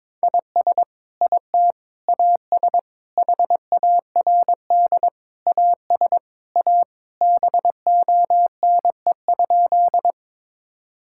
Morse code, heard as a keyed tone, about 21 wpm, 700 Hz. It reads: IS IT AS HARD AS A BONE?
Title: Sentences from Top 1600 Words 22wpm